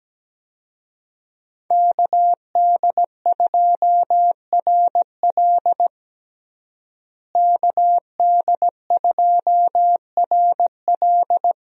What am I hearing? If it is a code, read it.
Morse code, heard as a keyed tone, about 17 wpm, 700 Hz. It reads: KD2RL KD2RL
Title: USA Call Signs - 5 Characters - Repeated 2x - 17wpm